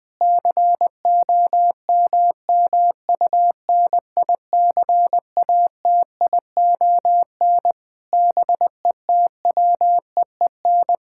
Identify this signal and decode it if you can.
Morse code, heard as a keyed tone, about 21 wpm, 700 Hz. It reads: COMMUNICATION BETWEEN
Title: Sentences from Top 1000 Words - Review Part 6 - 20wpm